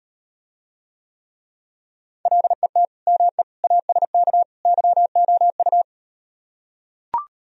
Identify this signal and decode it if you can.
Morse code, heard as a keyed tone, about 38 wpm, 700 Hz. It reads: LET ME ASK YOU